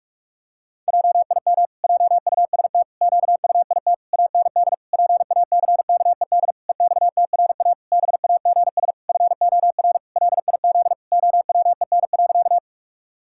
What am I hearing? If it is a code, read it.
Morse code, heard as a keyed tone, about 46 wpm, 700 Hz. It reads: JIM JUST QUIT AND PACKED EXTRA BAGS FOR LIZ OWEN.
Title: Pangram Challenge v2 - 45wpm